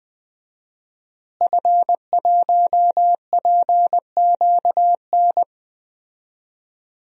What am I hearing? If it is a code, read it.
Morse code, heard as a keyed tone, about 20 wpm, 700 Hz. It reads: F1PQN
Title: DXCC-100 Call Signs - 5 Characters - Repeated 3x - 20wpm